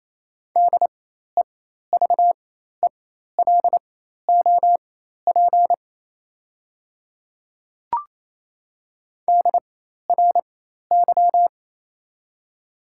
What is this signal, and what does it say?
Morse code, heard as a keyed tone, about 28 wpm, spaced as for 13 wpm, 700 Hz. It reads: DEVELOP DRY
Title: 4x Character Spacing - Top 500 Words - 28wpm